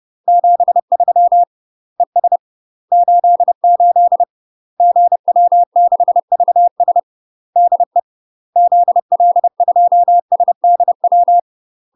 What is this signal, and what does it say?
Morse code, heard as a keyed tone, about 30 wpm, 700 Hz. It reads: 73 ES 88 GW6VS DE ZL2SDW
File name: UK MASTER QSOs TRAINING / 30WPM / UK MASTER QSOs 0001